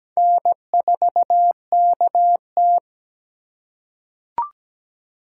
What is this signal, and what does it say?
Morse code, heard as a keyed tone, about 17 wpm, 700 Hz. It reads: N4KT
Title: USA Call Signs - 4 Characters - 4x Character Spacing - 17wpm